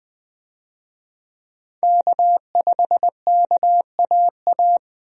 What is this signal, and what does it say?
Morse code, heard as a keyed tone, about 20 wpm, 700 Hz. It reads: K5KAA
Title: USA Call Signs - 5 Characters - Repeated 2x - 20wpm